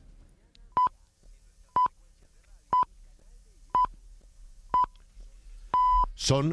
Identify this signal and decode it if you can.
Radio time signal (time pips), with speech. It is heard from 0.8 to 6.0 s.